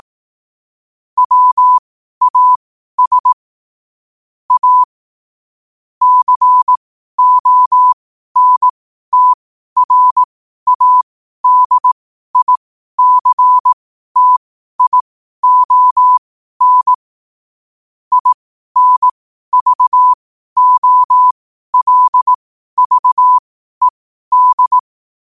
Morse code, heard as a keyed tone, about 18 wpm, spaced as for 12 wpm, 1000 Hz. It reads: WAS A CONTRADICTION INVOLVED
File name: qotdfiles / 12wpm